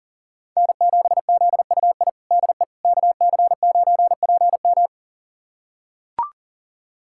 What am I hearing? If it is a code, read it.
Morse code, heard as a keyed tone, about 40 wpm, 700 Hz. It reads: N7ZUI DE KC9PM
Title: USA Callsign DE Callsign - 5 Characters - 40wpm